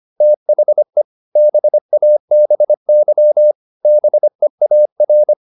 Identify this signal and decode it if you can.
Morse code, heard as a keyed tone, about 25 wpm, 600 Hz. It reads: THE BABY BEAR